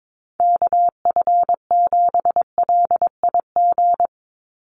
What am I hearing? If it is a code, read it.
Morse code, heard as a keyed tone, about 22 wpm, 700 Hz. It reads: KF7LIG
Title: USA Call Signs 22wpm